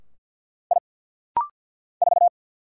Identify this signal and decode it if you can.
Morse code, heard as a keyed tone, about 49 wpm, 700 Hz. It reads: I 4
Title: Single Letter-Number - Rapid-Fire 50wpm